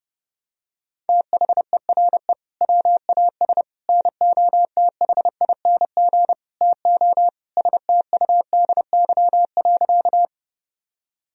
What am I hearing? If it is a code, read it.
Morse code, heard as a keyed tone, about 30 wpm, 700 Hz. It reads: THERE WAS NOTHING TO STUDY.
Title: Sentences from Top 1000 Words - Review Part 1 - 30wpm